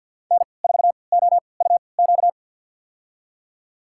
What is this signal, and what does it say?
Morse code, heard as a keyed tone, about 52 wpm, spaced as for 29 wpm, 700 Hz. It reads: N4OUQ
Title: USA Call Signs - 3x Character Spacing - 50wpm